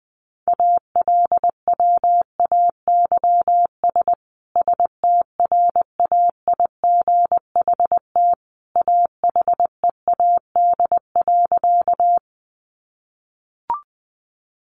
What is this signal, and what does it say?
Morse code, heard as a keyed tone, about 20 wpm, 700 Hz. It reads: ALWAYS STRAIGHT AHEAD.
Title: Sentences from Top 1100 Words 20wpm